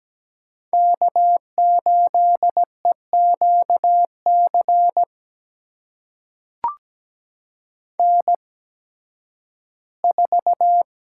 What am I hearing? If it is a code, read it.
Morse code, heard as a keyed tone, about 17 wpm, spaced as for 11 wpm, 700 Hz. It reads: K8EQC N4
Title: USA Call Signs - 5 Characters - 8x Character Spacing - 17wpm